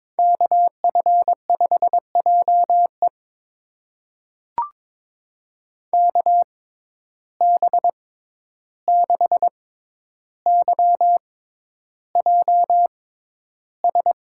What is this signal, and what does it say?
Morse code, heard as a keyed tone, about 22 wpm, spaced as for 11 wpm, 700 Hz. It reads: KF5JE KB6YJS
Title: USA Call Signs - 6x Character Spacing - 22wpm